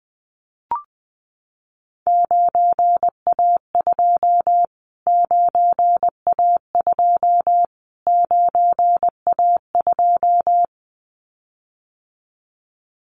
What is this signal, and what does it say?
Morse code, heard as a keyed tone, about 20 wpm, 700 Hz. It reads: 9A2 9A2 9A2
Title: Callsign Prefixes for Top 25 DXCC - Repeated 3x - 20wpm